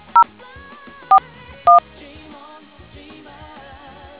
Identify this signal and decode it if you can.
DTMF, heard as telephone touch tones, tones of about 100 ms. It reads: 041